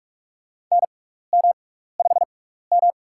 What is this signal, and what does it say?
Morse code, heard as a keyed tone, about 46 wpm, spaced as for 16 wpm, 700 Hz. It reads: NM5M
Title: USA Call Signs - 6x Character Spacing - 45wpm